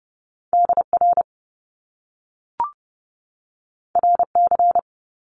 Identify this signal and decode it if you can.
Morse code, heard as a keyed tone, about 30 wpm, 700 Hz. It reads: DR RC
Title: Top Two Letter Combinations in Words 30wpm